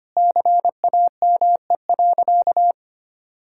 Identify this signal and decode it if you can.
Morse code, heard as a keyed tone, about 25 wpm, 700 Hz. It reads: CAME.
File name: practice / 25wpm / Sentences with 3-Word Phrases - 25wpm